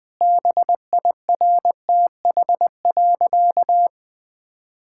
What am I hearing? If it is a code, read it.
Morse code, heard as a keyed tone, about 20 wpm, 700 Hz. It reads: BIRTH.
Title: Sentences from Top 1000 Words - Review Part 5 - 20wpm